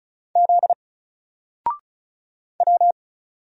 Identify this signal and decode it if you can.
Morse code, heard as a keyed tone, about 36 wpm, 700 Hz. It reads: Z W
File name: Single-Letter-Number-35wpm